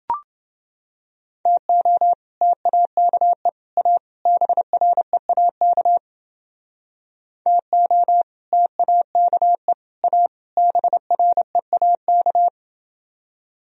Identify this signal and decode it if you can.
Morse code, heard as a keyed tone, about 29 wpm, 700 Hz. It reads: TO TAKE A BREAK TO TAKE A BREAK